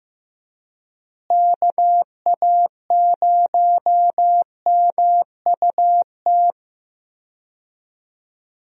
Morse code, heard as a keyed tone, about 16 wpm, 700 Hz. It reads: KA0MUT
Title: USA Call Signs - 6 Characters - Repeated 3x - 15wpm